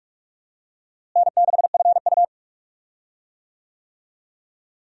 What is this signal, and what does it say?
Morse code, heard as a keyed tone, about 45 wpm, 700 Hz. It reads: N6FU
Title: USA Call Signs - 4 Characters - Repeated 3x - 45wpm